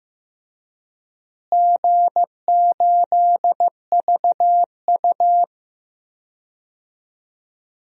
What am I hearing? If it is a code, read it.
Morse code, heard as a keyed tone, about 15 wpm, 700 Hz. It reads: G8VU
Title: DXCC-100 Call Signs - 4 Characters - Repeated 2x - 15wpm